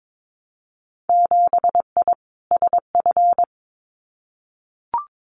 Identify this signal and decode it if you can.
Morse code, heard as a keyed tone, about 22 wpm, 700 Hz. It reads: 7I SF
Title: Winter Field Day Exchanges Encore 22wpm